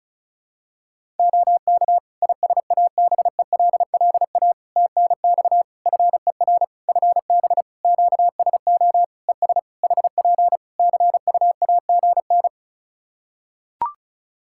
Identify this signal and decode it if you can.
Morse code, heard as a keyed tone, about 36 wpm, 700 Hz. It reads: OK ISABELLA TNX FER FB QSO ES HP CUAGN
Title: Practice QSO - Exchange 4 - Ending - 35wpm